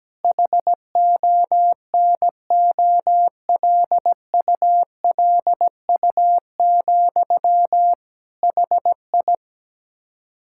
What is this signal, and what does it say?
Morse code, heard as a keyed tone, about 17 wpm, 700 Hz. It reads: HONOLULU, HI
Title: Practice QSO - Exchange 1 - Beginning - 17wpm